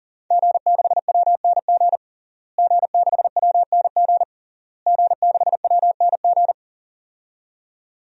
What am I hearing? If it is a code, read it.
Morse code, heard as a keyed tone, about 40 wpm, 700 Hz. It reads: G6WNG G6WNG G6WNG
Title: DXCC-100 Call Signs - 5 Characters - Repeated 3x - 40wpm